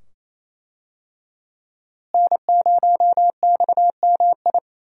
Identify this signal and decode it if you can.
Morse code, heard as a keyed tone, about 28 wpm, 700 Hz. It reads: N0XMI
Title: USA Call Signs - 5 Characters - Repeated 2x - 28wpm